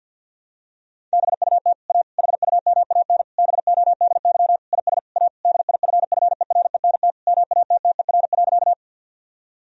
Morse code, heard as a keyed tone, about 52 wpm, 700 Hz. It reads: BUT A HUMAN BODY IS A DIFFERENT MATTER.
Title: Sentences from Top 1000 Words - Review Part 3 - 50wpm